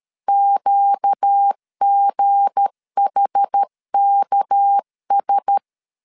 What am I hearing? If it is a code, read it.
Morse code, heard as a keyed tone, about 13 wpm, 800 Hz. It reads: QGHKS
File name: LEZIONI / 9 286 12wpm 2305